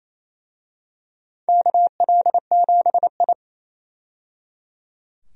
Morse code, heard as a keyed tone, about 27 wpm, 700 Hz. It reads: KL7I
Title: DXCC-100 Call Signs - 4 Characters - 28wpm